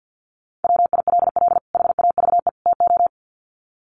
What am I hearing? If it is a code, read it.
Morse code, heard as a keyed tone, about 50 wpm, 700 Hz. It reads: WILL HAVE TO
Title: Sets of 3 Words 50wpm